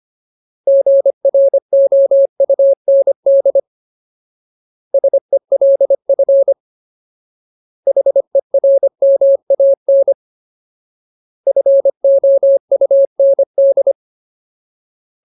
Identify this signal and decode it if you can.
Morse code, heard as a keyed tone, about 25 wpm, 550 Hz. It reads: GROUND SELF HERMAN FOUND
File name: Ultimate Word List / 25WPM Ultimate